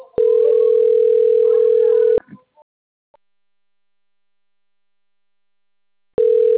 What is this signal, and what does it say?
Ringback tone (North America), 440 + 480 Hz.
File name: rg-1600-unknown- - - .38784